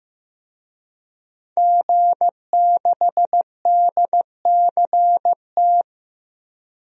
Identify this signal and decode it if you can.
Morse code, heard as a keyed tone, about 16 wpm, 700 Hz. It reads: G6DCT
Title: DXCC-100 Call Signs - 5 Characters - 15wpm